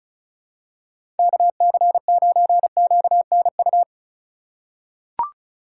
Morse code, heard as a keyed tone, about 36 wpm, 700 Hz. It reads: KC9QNU
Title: USA Call Signs - 4x Character Spacing - 35wpm